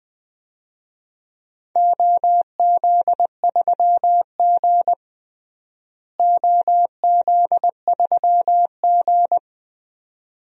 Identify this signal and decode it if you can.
Morse code, heard as a keyed tone, about 20 wpm, 700 Hz. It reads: OZ3G OZ3G
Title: DXCC-100 Call Signs - 4 Characters - Repeated 2x - 20wpm